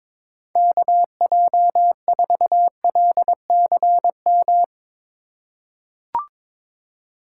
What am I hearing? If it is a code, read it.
Morse code, heard as a keyed tone, about 22 wpm, 700 Hz. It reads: KJ4LCM
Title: USA Call Signs - 6 Characters - 2x Character Spacing - 22wpm